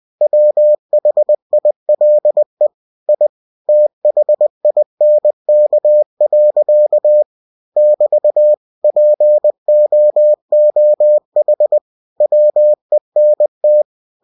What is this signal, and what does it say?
Morse code, heard as a keyed tone, about 20 wpm, 600 Hz. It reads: WHILE I THINK. <BT> POOH WENT